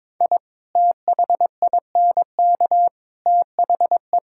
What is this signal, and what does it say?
Morse code, heard as a keyed tone, about 22 wpm, 700 Hz. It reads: I THINK THE
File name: practice / 22wpm / Sentences with 3-Word Phrases - 22wpm